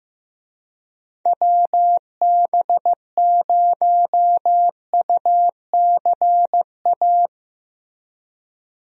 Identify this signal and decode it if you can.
Morse code, heard as a keyed tone, about 15 wpm, 700 Hz. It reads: WB0UCA
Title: USA Call Signs - 6 Characters - Repeated 2x - 15wpm